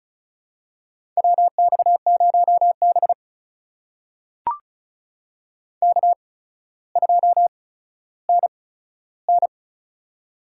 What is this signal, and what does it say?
Morse code, heard as a keyed tone, about 36 wpm, spaced as for 15 wpm, 700 Hz. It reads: WX0B K2NN
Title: USA Call Signs - 4 Characters - 8x Character Spacing - 35wpm